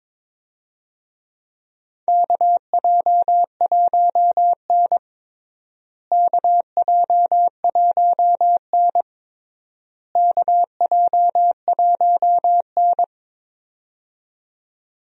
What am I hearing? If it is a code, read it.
Morse code, heard as a keyed tone, about 22 wpm, 700 Hz. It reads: KJ1N KJ1N KJ1N
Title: USA Call Signs - 4 Characters - Repeated 3x - 22wpm